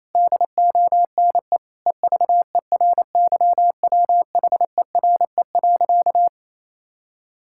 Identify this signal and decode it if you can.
Morse code, heard as a keyed tone, about 28 wpm, 700 Hz. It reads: DONE EVERYWHERE.
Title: Sentences from Top 1600 Words 28wpm